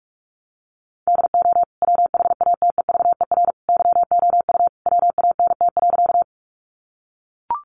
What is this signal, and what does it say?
Morse code, heard as a keyed tone, about 44 wpm, 700 Hz. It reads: DO WHATEVER YOU WANT.
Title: Sentences from Top 1100 Words 45wpm